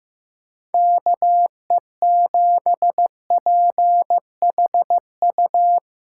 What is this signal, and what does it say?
Morse code, heard as a keyed tone, about 15 wpm, 700 Hz. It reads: KE7PHU
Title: USA Callsign DE Callsign - 6 Characters - 15wpm